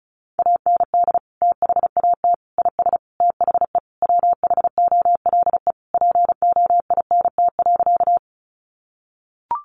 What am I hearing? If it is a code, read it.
Morse code, heard as a keyed tone, about 34 wpm, 700 Hz. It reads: AND THAT IS THE WHOLE POINT.
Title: Sentences from Top 400 Words 35wpm